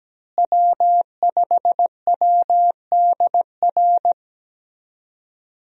Morse code, heard as a keyed tone, about 17 wpm, 700 Hz. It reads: W5WDR
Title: USA Call Signs - 5 Characters - Repeated 2x - 17wpm